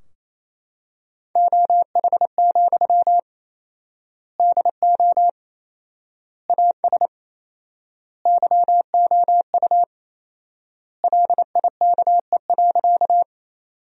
Morse code, heard as a keyed tone, about 28 wpm, 700 Hz. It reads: OH, DO AS YOU LIKE.
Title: 4x Word-Spacing - Sentences from Top 500 Words - 28wpm